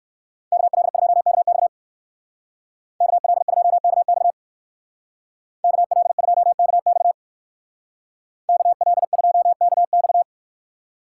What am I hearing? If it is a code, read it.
Morse code, heard as a keyed tone, about 60 wpm, 700 Hz. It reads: KL2KX KL2KX KL2KX KL2KX